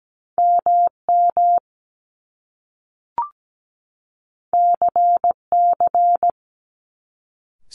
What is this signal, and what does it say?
Morse code, heard as a keyed tone, about 17 wpm, 700 Hz. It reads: MM CC
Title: Top Two Letter Combinations in Words 17wpm